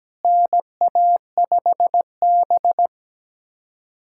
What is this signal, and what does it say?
Morse code, heard as a keyed tone, about 17 wpm, 700 Hz. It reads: NA5B
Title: USA Callsign DE Callsign - 4 Characters - 17wpm